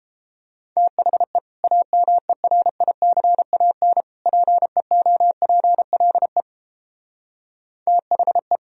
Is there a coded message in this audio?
Yes: Morse code from 0.8 to 8.6 s, heard as a keyed tone, about 29 wpm, 700 Hz. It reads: THE AMERICAN PEOPLE THE